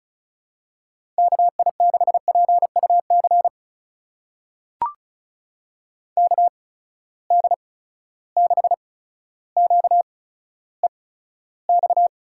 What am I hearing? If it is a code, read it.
Morse code, heard as a keyed tone, about 36 wpm, spaced as for 15 wpm, 700 Hz. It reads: KI6PUC KD6QEX
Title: USA Call Signs - 6 Characters - 8x Character Spacing - 35wpm